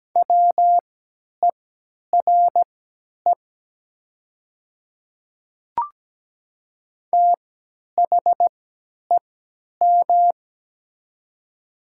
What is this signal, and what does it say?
Morse code, heard as a keyed tone, about 17 wpm, spaced as for 10 wpm, 700 Hz. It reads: WERE THEM
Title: 3x Character Spacing - Top 100 Words - 17wpm